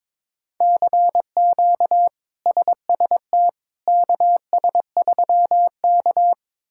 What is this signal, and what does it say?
Morse code, heard as a keyed tone, about 22 wpm, 700 Hz. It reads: CQ SST KS3K
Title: K1USN SST QSO Practice - 22wpm